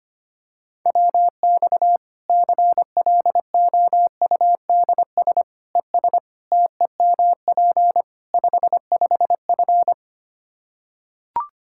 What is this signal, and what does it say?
Morse code, heard as a keyed tone, about 25 wpm, 700 Hz. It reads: WX CLOUDS ES TEMP 55F
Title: K1USN SST QSO Practice 25wpm